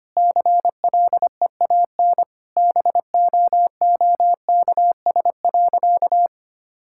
Morse code, heard as a keyed tone, about 25 wpm, 700 Hz. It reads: CLEAN BOOKS.
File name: practice / 25wpm / Sentences with 4-Word Phrases - 25wpm